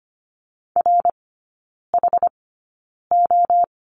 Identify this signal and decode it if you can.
Morse code, heard as a keyed tone, about 25 wpm, spaced as for 9 wpm, 700 Hz. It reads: RHO
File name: Lesson 15e - Number 5 - Three Characters 25wpm